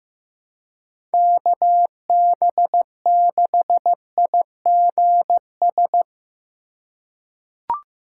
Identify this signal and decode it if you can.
Morse code, heard as a keyed tone, about 15 wpm, 700 Hz. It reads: KB6IGS
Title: USA Call Signs - 6 Characters - 3x Character Spacing - 15wpm